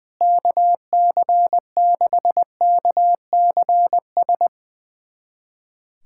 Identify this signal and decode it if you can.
Morse code, heard as a keyed tone, about 21 wpm, 700 Hz. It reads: KC6KCS
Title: USA Callsign DE Callsign - 6 Characters - 20wpm